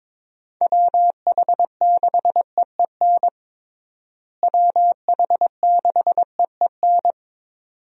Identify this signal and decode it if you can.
Morse code, heard as a keyed tone, about 22 wpm, 700 Hz. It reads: WH6EEN WH6EEN
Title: USA Call Signs - 6 Characters - Repeated 2x - 22wpm